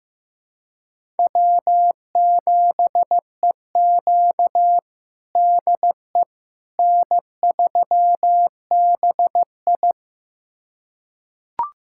Morse code, heard as a keyed tone, about 15 wpm, 700 Hz. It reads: W7EQ DE N3BI
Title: USA Callsign DE Callsign - 4 Characters - 15wpm